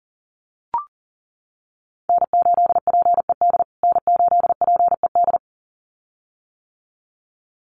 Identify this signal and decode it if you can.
Morse code, heard as a keyed tone, about 37 wpm, 700 Hz. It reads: N8PED N8PED
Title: USA Call Signs Repeated 2x 40wpm